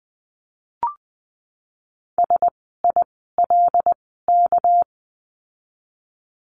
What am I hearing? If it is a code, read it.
Morse code, heard as a keyed tone, about 20 wpm, spaced as for 15 wpm, 700 Hz. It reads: SILK
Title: 199 Lesson 32f - Letter k - Words 20wpm